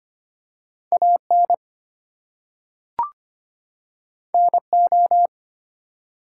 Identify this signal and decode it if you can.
Morse code, heard as a keyed tone, about 25 wpm, 700 Hz. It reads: AN NO
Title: Two Letter Words - 25wpm